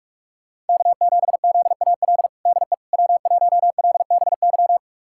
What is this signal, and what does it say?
Morse code, heard as a keyed tone, about 41 wpm, 700 Hz. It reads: K7ZAL DE W1LBY
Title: USA Callsign DE Callsign - 5 Characters - 45wpm